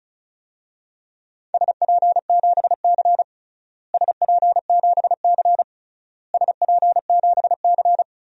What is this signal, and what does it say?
Morse code, heard as a keyed tone, about 36 wpm, 700 Hz. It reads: SP7C SP7C SP7C
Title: DXCC-100 Call Signs - 4 Characters - Repeated 3x - 35wpm